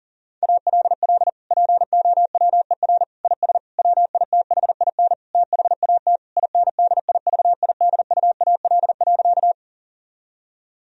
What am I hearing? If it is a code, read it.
Morse code, heard as a keyed tone, about 37 wpm, 700 Hz. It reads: ALL POWER IS WITHIN THAT INDIVIDUAL.
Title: Sentences from Top 1000 Words - Review Part 4 - 40wpm